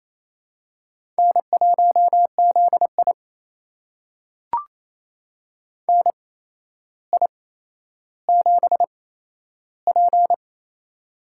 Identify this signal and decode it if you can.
Morse code, heard as a keyed tone, about 28 wpm, spaced as for 12 wpm, 700 Hz. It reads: N1ZI NI7P
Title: USA Call Signs - 4 Characters - 8x Character Spacing - 28wpm